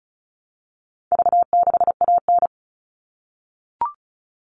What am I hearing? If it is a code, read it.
Morse code, heard as a keyed tone, about 34 wpm, 700 Hz. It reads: V6AN
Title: All Cut Numbers - 4-Digit 35wpm